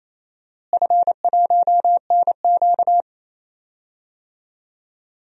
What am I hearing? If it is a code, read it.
Morse code, heard as a keyed tone, about 28 wpm, 700 Hz. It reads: F1NQ
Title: DXCC-100 Call Signs - 4 Characters - Repeated 2x - 28wpm